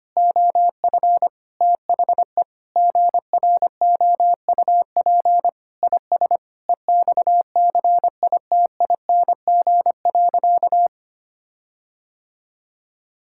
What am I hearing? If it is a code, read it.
Morse code, heard as a keyed tone, about 25 wpm, 700 Hz. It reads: OF THE GROUP IS EXCITING.
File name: practice / 25wpm / Sentences with 4-Word Phrases - 25wpm